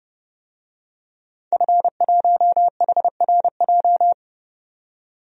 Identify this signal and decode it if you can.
Morse code, heard as a keyed tone, about 30 wpm, 700 Hz. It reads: F1HRJ
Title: DXCC-100 Call Signs - 5 Characters - 30wpm